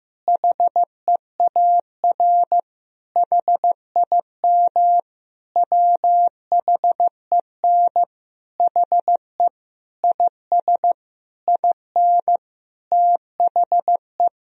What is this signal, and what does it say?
Morse code, heard as a keyed tone, about 15 wpm, 700 Hz. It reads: HEAR HIM WHEN HE IS IN THE
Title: Sentences from Top 1000 Words - Review Part 2 - 15wpm